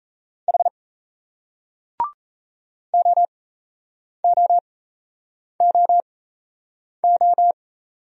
Morse code, heard as a keyed tone, about 38 wpm, spaced as for 8 wpm, 700 Hz. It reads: H OOOO